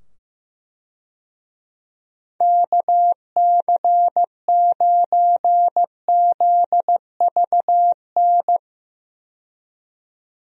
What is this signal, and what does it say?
Morse code, heard as a keyed tone, about 15 wpm, 700 Hz. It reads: KC9ZVN
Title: USA Call Signs - 6 Characters - 15wpm